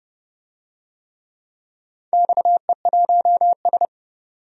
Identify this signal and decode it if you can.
Morse code, heard as a keyed tone, about 29 wpm, 700 Hz. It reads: XE1S
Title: DXCC-100 Call Signs - 4 Characters - 30wpm